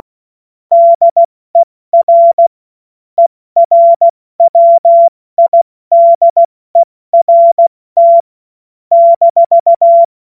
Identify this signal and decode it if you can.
Morse code, heard as a keyed tone, about 16 wpm, 680 Hz. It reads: DER ERWIDERT -
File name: Witz-26-August-2025-70-F80